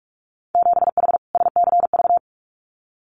Morse code, heard as a keyed tone, about 44 wpm, 700 Hz. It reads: 7H SCV
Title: Winter Field Day Exchanges Encore 45wpm